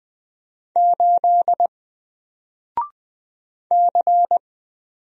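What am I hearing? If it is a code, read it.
Morse code, heard as a keyed tone, about 20 wpm, 700 Hz. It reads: 8 C
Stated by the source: Single letter and number Morse drill at 20 wpm, from N4LCD.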